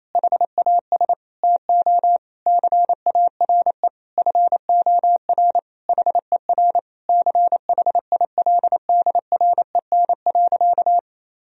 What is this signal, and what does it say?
Morse code, heard as a keyed tone, about 28 wpm, 700 Hz. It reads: HAS TO CARE FOR HER CHILDREN.
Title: Sentences from Top 1000 Words - Review Part 3 - 28wpm